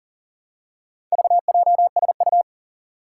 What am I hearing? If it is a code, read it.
Morse code, heard as a keyed tone, about 40 wpm, 700 Hz. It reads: VJSU
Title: Four Challenging Letter-Number Combinations 40wpm